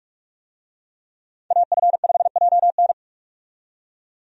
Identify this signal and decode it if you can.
Morse code, heard as a keyed tone, about 46 wpm, 700 Hz. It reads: AF5JN
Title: USA Call Signs - 5 Characters - 45wpm